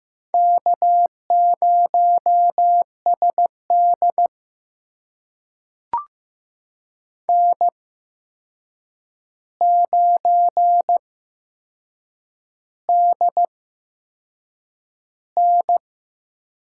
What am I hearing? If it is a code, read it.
Morse code, heard as a keyed tone, about 15 wpm, spaced as for 6 wpm, 700 Hz. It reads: K0SD N9DN
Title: USA Call Signs - 4 Characters - 8x Character Spacing - 15wpm